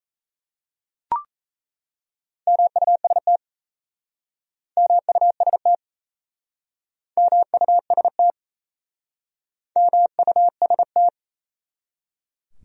Morse code, heard as a keyed tone, about 38 wpm, 700 Hz. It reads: MUST MUST MUST MUST